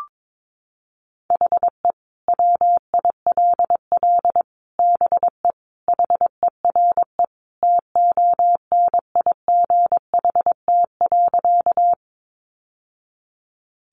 Morse code, heard as a keyed tone, about 22 wpm, 700 Hz. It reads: HE WILL BE HERE TONIGHT.
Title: Sentences from Top 600 Words 22wpm